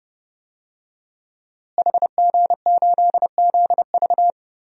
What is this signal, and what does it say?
Morse code, heard as a keyed tone, about 30 wpm, 700 Hz. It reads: HG8ZV
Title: DXCC-100 Call Signs - 5 Characters - Repeated 3x - 30wpm